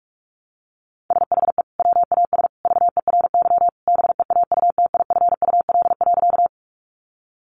Morse code, heard as a keyed tone, about 44 wpm, 700 Hz. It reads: SHE WAS VERY BEAUTIFUL.
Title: Sentences from Top 900 Words 45wpm